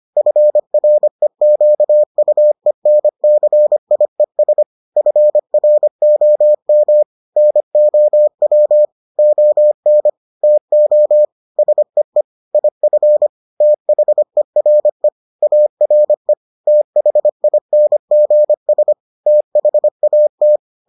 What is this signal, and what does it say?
Morse code, heard as a keyed tone, about 25 wpm, 600 Hz. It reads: FREQUENCIES FROM NOW ON TO SEE IF THERE ARE THINGS THAT